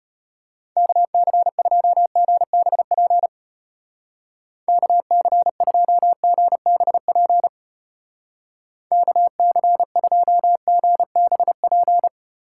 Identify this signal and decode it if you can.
Morse code, heard as a keyed tone, about 32 wpm, 700 Hz. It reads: KC2GBP KC2GBP KC2GBP